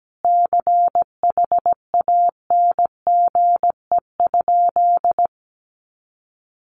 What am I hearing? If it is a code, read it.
Morse code, heard as a keyed tone, about 17 wpm, 700 Hz. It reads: CHANGE?
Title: Sentences from Top 1500 Words 17wpm